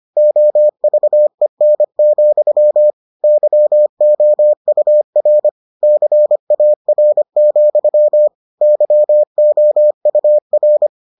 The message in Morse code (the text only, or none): OVEN, YOUR CAR, YOUR